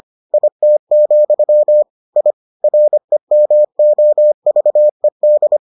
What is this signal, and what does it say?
Morse code, heard as a keyed tone, about 25 wpm, 600 Hz. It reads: IT, I REMOVED